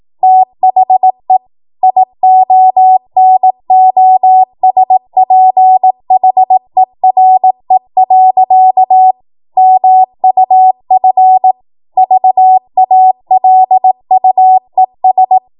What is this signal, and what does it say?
Morse code, heard as a keyed tone, about 18 wpm, 750 Hz. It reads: THE IONOSPHERE. MUF VALUES
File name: Morse / 18WPM